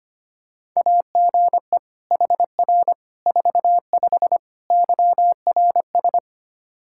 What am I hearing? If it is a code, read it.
Morse code, heard as a keyed tone, about 25 wpm, 700 Hz. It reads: AGE HR 45 YRS
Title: K1USN SST QSO Practice 25wpm